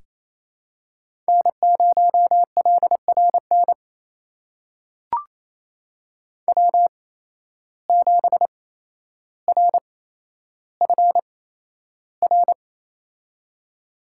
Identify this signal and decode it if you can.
Morse code, heard as a keyed tone, about 28 wpm, spaced as for 12 wpm, 700 Hz. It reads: N0LRN W7RFR
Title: USA Call Signs - 5 Characters - 8x Character Spacing - 28wpm